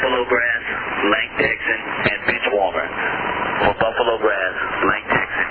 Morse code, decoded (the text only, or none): none